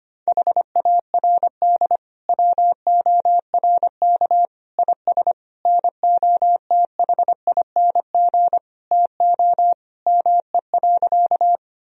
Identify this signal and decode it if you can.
Morse code, heard as a keyed tone, about 25 wpm, 700 Hz. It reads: HARD WORK IS NOTHING TO ME.
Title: Sentences from Top 1000 Words - Review Part 1 - 25wpm